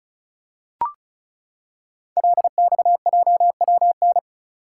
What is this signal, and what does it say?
Morse code, heard as a keyed tone, about 36 wpm, 700 Hz. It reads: LXJWN
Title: Code Groups - 5 Letter:Numbers - 35wpm